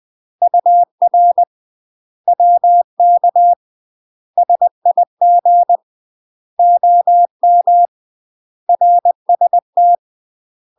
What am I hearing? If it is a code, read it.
Morse code, heard as a keyed tone, about 20 wpm, 700 Hz. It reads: UR WK SIG OM RST
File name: UK MASTER QSOs TRAINING / 20WPM / UK MASTER QSOs 0051